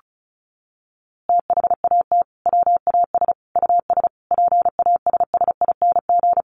Morse code, heard as a keyed tone, about 34 wpm, 700 Hz. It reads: THAT WAS US PASSING
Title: A Sound Of Thunder 35wpm